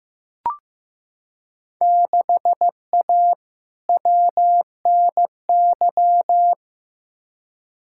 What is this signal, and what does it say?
Morse code, heard as a keyed tone, about 15 wpm, 700 Hz. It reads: 6A WNY
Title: Field Day Exchanges - 15wpm